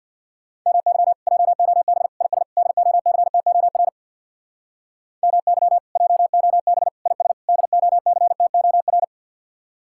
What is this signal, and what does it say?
Morse code, heard as a keyed tone, about 60 wpm, 700 Hz. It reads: MY JOB IS DOCTOR MY JOB IS DOCTOR
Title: Speed-Racing Common QSO Phrases - 50wpm